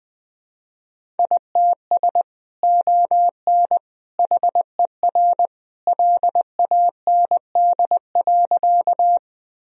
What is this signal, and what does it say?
Morse code, heard as a keyed tone, about 21 wpm, 700 Hz. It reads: ITS ON HER LAND.
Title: Sentences from Top 1000 Words - Review Part 1 - 20wpm